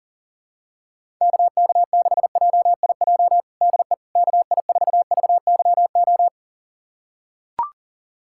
Morse code, heard as a keyed tone, about 37 wpm, 700 Hz. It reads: KK6JIJ DE KI4VYO
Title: USA Callsign DE Callsign - 6 Characters - 40wpm